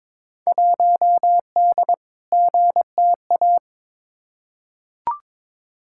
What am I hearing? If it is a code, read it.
Morse code, heard as a keyed tone, about 22 wpm, 700 Hz. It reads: 1D GTA
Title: Field Day Exchanges - 22wpm